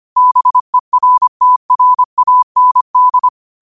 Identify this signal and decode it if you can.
Morse code, heard as a keyed tone, about 25 wpm, 1000 Hz. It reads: BERTRAND
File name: qotdfiles / 25wpm